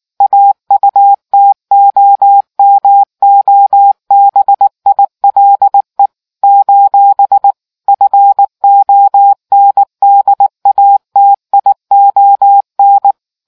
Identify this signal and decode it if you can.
Morse code, heard as a keyed tone, about 19 wpm, 800 Hz. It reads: AUTOMOBILE : FONDATION